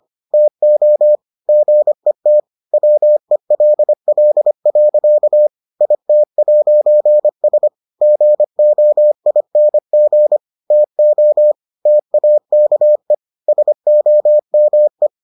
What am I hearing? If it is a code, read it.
Morse code, heard as a keyed tone, about 25 wpm, 600 Hz. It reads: TO GET WELL. IT'S GOING TO TAKE SOME